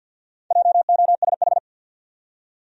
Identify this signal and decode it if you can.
Morse code, heard as a keyed tone, about 51 wpm, 700 Hz. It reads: JOSH
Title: USA Common Names - Max 5 Letters - 50wpm